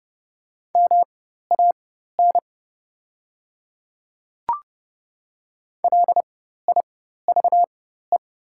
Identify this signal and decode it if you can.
Morse code, heard as a keyed tone, about 29 wpm, spaced as for 14 wpm, 690 Hz. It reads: MAN LIVE
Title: 4x Character Spacing - Top 200 Words - 30wpm